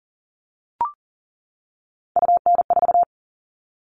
Morse code, heard as a keyed tone, about 40 wpm, 700 Hz. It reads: UN4
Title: All Cut Numbers - 3-Digit 40wpm